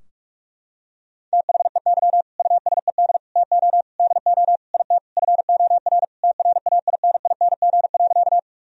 Morse code, heard as a keyed tone, about 44 wpm, 700 Hz. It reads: THEY USED TO DO IT FOR TRAINING.
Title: Sentences from Top 1000 Words - Review Part 4 - 45wpm